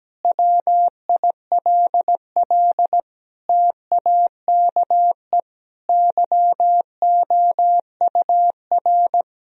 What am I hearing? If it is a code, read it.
Morse code, heard as a keyed tone, about 17 wpm, 700 Hz. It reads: WILL TAKE YOUR
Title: Sentences from Top 1000 Words - Review Part 4 - 17wpm